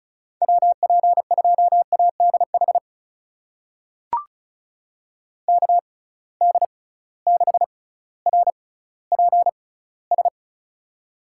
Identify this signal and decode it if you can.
Morse code, heard as a keyed tone, about 36 wpm, spaced as for 18 wpm, 700 Hz. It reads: WP2ADH KD6RPS
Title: USA Call Signs - 6 Characters - 6x Character Spacing - 35wpm